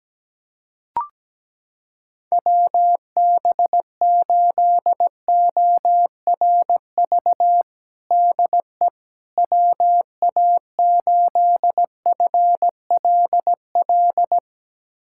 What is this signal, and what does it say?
Morse code, heard as a keyed tone, about 17 wpm, 700 Hz. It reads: WB8ORV DE WA8FLL
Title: USA Callsign DE Callsign - 6 Characters - 17wpm